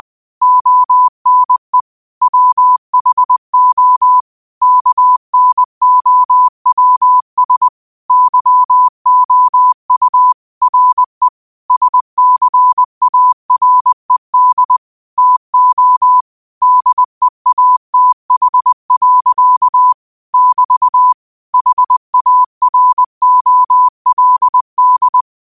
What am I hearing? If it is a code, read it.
Morse code, heard as a keyed tone, about 20 wpm, 1000 Hz. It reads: ONE WHO KNOWS YOU RE SCARED TO DEATH. <BT> HAROLD
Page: Quotes for Tue, 12 Aug 2025 in Morse Code at 20 words per minute.